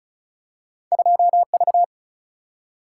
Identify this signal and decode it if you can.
Morse code, heard as a keyed tone, about 36 wpm, 700 Hz. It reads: 2 V
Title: Single-Letter Single-Number 35wpm